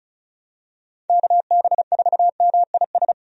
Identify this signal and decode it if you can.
Morse code, heard as a keyed tone, about 35 wpm, 700 Hz. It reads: KB4MIS
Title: USA Call Signs - 6 Characters - 35wpm